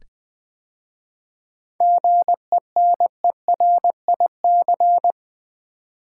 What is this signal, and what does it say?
Morse code, heard as a keyed tone, about 21 wpm, 700 Hz. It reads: GENERIC
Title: 7 Letter Words - First 3 Letters Given - 20wpm